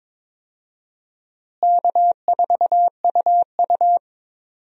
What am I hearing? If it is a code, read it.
Morse code, heard as a keyed tone, about 22 wpm, 700 Hz. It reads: K4UU
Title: USA Call Signs - 4 Characters - Repeated 2x - 22wpm